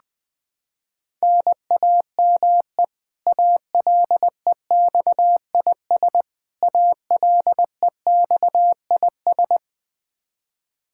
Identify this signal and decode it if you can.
Morse code, heard as a keyed tone, about 20 wpm, 700 Hz. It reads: NAME ALEXIS ALEXIS
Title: K1USN SST QSO Practice 20wpm